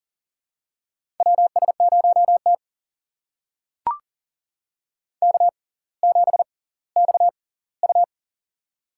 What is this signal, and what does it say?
Morse code, heard as a keyed tone, about 40 wpm, spaced as for 21 wpm, 700 Hz. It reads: WS0T K7XU
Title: USA Call Signs - 4 Characters - 6x Character Spacing - 40wpm